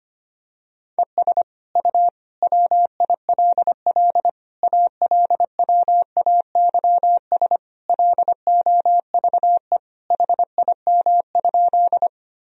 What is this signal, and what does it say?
Morse code, heard as a keyed tone, about 25 wpm, 700 Hz. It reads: ES U WILL ALWAYS LOVE HIM?
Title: Sentences from Top 400 Words and Abbreviations 25wpm